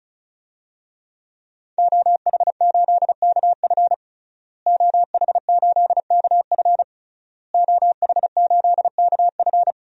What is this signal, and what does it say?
Morse code, heard as a keyed tone, about 34 wpm, 700 Hz. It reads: OH8KF OH8KF OH8KF
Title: DXCC-100 Call Signs - 5 Characters - Repeated 3x - 35wpm